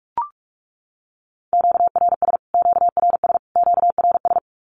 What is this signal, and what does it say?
Morse code, heard as a keyed tone, about 44 wpm, 700 Hz. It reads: QRS QRS QRS
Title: Q-Codes Repeated 3x 45wpm